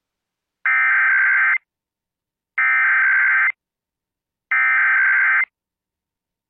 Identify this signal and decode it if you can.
Emergency Alert System (SAME) tones (header bursts), heard as harsh buzzing bursts. It is heard from 0.7 to 5.4 s.